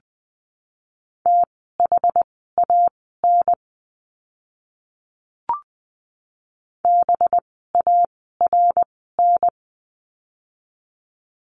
Morse code, heard as a keyed tone, about 20 wpm, spaced as for 14 wpm, 700 Hz. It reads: THAN BARN
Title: 185 Lesson 30f - Letter b - Words 20wpm